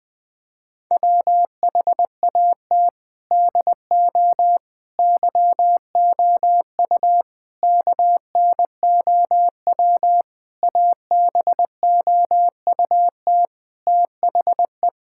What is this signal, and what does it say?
Morse code, heard as a keyed tone, about 19 wpm, 700 Hz. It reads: WHAT DO YOU KNOW ABOUT THE
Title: Sentences from Top 1000 Words - Review Part 6 - 20wpm